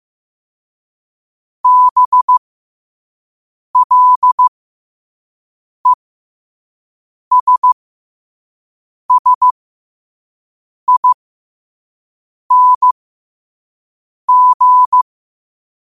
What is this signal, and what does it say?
Morse code, heard as a keyed tone, about 15 wpm, spaced as for 5 wpm, 1000 Hz. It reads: BLESSING
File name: qotdfiles / 5wpm